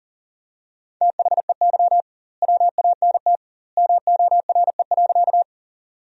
Morse code, heard as a keyed tone, about 40 wpm, 700 Hz. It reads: THEY WANT MORE.
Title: 2x Word-Spacing - Sentences from Top 200 Words - 40wpm